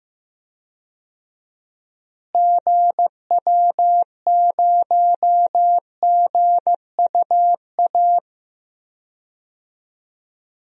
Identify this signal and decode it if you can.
Morse code, heard as a keyed tone, about 14 wpm, 700 Hz. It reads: GW0GUA
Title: DXCC-100 Call Signs - 6 Characters - Repeated 2x - 15wpm